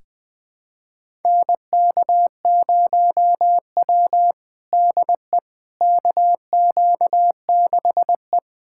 Morse code, heard as a keyed tone, about 20 wpm, 700 Hz. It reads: NK0W DE KQ6E
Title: USA Callsign DE Callsign - 4 Characters - 20wpm